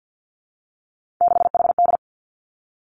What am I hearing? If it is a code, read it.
Morse code, heard as a keyed tone, about 50 wpm, 700 Hz. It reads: 6HD
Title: Three Challenging Letter-Number Combinations 50wpm